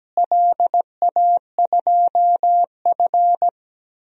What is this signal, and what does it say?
Morse code, heard as a keyed tone, about 17 wpm, 700 Hz. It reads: LA2F
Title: DXCC-100 Call Signs - 4 Characters - Repeated 2x - 17wpm